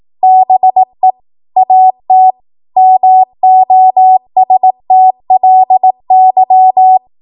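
Morse code, heard as a keyed tone, about 18 wpm, 750 Hz. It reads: BE AT MOSTLY
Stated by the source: 18 WPM Morse code practice from a national amateur radio association's code archive.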